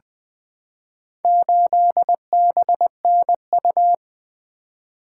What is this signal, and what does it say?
Morse code, heard as a keyed tone, about 20 wpm, 700 Hz. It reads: 8BNU
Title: Four Letter-Number Combinations - 20wpm